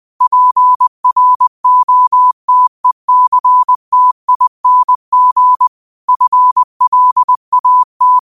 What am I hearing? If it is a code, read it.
Morse code, heard as a keyed tone, about 20 wpm, 1000 Hz. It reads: PROTECTING FLAT
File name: qotdfiles / 20wpm